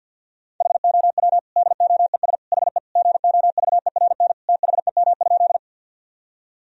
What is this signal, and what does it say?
Morse code, heard as a keyed tone, about 50 wpm, 700 Hz. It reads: HOW DOES HE GOVERN THEM?
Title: Sentences from Top 1000 Words - Review Part 2 - 50wpm